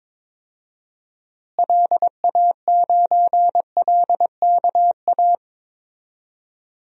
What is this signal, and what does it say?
Morse code, heard as a keyed tone, about 22 wpm, 700 Hz. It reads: LA9LKA
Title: DXCC-100 Call Signs - 6 Characters - Repeated 2x - 22wpm